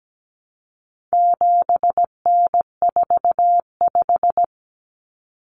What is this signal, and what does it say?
Morse code, heard as a keyed tone, about 17 wpm, 700 Hz. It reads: 7N45
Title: Common Cut Numbers - 4-Digit 17wpm